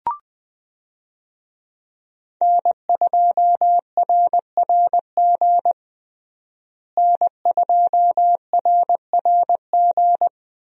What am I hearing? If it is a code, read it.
Morse code, heard as a keyed tone, about 20 wpm, 700 Hz. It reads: N2RRG N2RRG
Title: USA Call Signs - 5 Characters - Repeated 3x - 20wpm